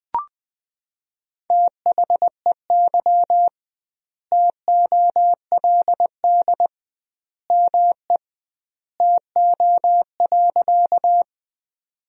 Morse code, heard as a keyed tone, about 20 wpm, 700 Hz. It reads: THEY TOLD ME TO.
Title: 2x Word-Spacing - Sentences from Top 400 Words - 20wpm